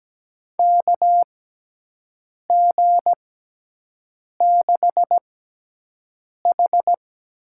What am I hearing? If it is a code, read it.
Morse code, heard as a keyed tone, about 17 wpm, spaced as for 6 wpm, 700 Hz. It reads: KG6H
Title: USA Call Signs - 6 Characters - 6x Character Spacing - 17wpm